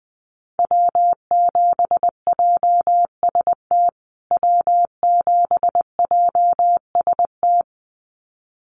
Morse code, heard as a keyed tone, about 20 wpm, 700 Hz. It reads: W7JST W7JST
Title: USA Call Signs Repeated 2x 20wpm